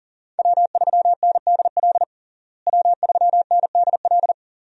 Morse code, heard as a keyed tone, about 40 wpm, 700 Hz. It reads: W3NDL W3NDL
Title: USA Call Signs - 5 Characters - Repeated 3x - 40wpm